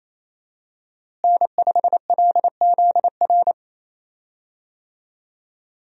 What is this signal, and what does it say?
Morse code, heard as a keyed tone, about 27 wpm, 700 Hz. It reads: N5LZR
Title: USA Call Signs - 5 Characters - Repeated 3x - 28wpm